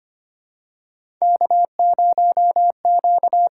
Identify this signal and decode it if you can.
Morse code, heard as a keyed tone, about 25 wpm, 700 Hz. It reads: K0Q
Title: USA Call Signs - 4 Characters - 6x Character Spacing - 25wpm